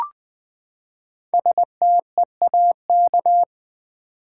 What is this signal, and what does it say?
Morse code, heard as a keyed tone, about 20 wpm, 700 Hz. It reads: STEAK
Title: Five Letter Words - 20wpm